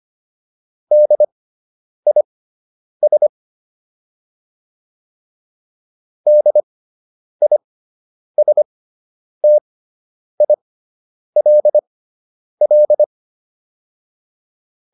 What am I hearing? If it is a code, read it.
Morse code, heard as a keyed tone, about 25 wpm, spaced as for 9 wpm, 600 Hz. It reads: DIS DISTILL